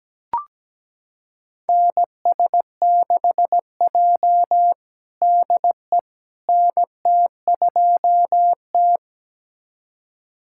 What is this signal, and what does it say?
Morse code, heard as a keyed tone, about 17 wpm, 700 Hz. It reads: NS6J DE NT2T
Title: USA Callsign DE Callsign - 4 Characters - 17wpm